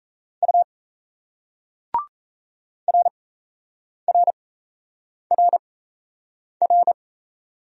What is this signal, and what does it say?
Morse code, heard as a keyed tone, about 35 wpm, spaced as for 8 wpm, 700 Hz. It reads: U RRRR